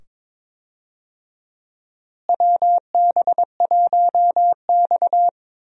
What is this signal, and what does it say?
Morse code, heard as a keyed tone, about 21 wpm, 700 Hz. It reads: WB1X
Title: USA Call Signs - 4 Characters - Repeated 2x - 22wpm